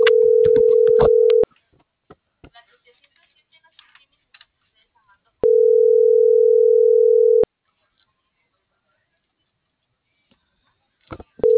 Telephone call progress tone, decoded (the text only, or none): ringback tone